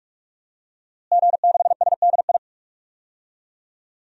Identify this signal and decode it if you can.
Morse code, heard as a keyed tone, about 44 wpm, 700 Hz. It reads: G6SDI
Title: DXCC-100 Call Signs - 5 Characters - Repeated 2x - 45wpm